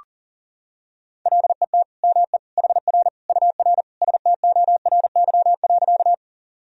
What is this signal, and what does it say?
Morse code, heard as a keyed tone, about 40 wpm, 700 Hz. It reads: LET ME HR UR STORY.
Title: Sentences from Top 300 Words and Abbreviations 40wpm